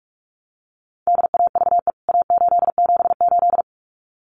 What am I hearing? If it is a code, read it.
Morse code, heard as a keyed tone, about 44 wpm, 700 Hz. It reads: DAVE A878
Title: CWT Exchange 45wpm